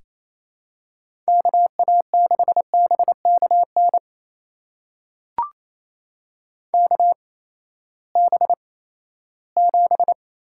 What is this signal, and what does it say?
Morse code, heard as a keyed tone, about 27 wpm, spaced as for 16 wpm, 700 Hz. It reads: KA6BKN KB7
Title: USA Call Signs - 6 Characters - 8x Character Spacing - 28wpm